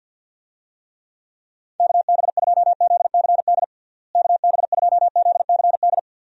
Morse code, heard as a keyed tone, about 50 wpm, 700 Hz. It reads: KB2ZCD KB2ZCD
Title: USA Call Signs - 6 Characters - Repeated 2x - 50wpm